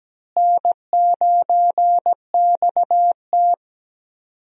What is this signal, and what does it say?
Morse code, heard as a keyed tone, about 17 wpm, 700 Hz. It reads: N9XT
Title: USA Call Signs - 4 Characters - Repeated 3x - 17wpm